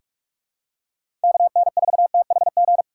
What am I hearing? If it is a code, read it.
Morse code, heard as a keyed tone, about 48 wpm, 700 Hz. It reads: KN4THG
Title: USA Call Signs - 4x Character Spacing - 45wpm